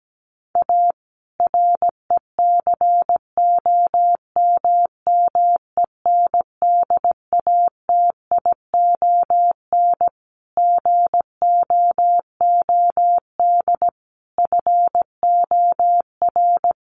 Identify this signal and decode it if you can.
Morse code, heard as a keyed tone, about 17 wpm, 700 Hz. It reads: A RECOMMENDATION GOOD FOR
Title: Sentences from Top 1400 Words 17wpm